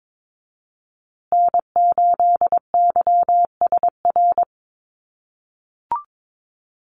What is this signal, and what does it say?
Morse code, heard as a keyed tone, about 22 wpm, 700 Hz. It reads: N8YSR
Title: USA Call Signs Repeated 3x 22wpm